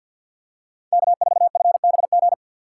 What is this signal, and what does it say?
Morse code, heard as a keyed tone, about 47 wpm, 690 Hz. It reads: K4FBG
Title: USA Call Signs - 3x Character Spacing - 50wpm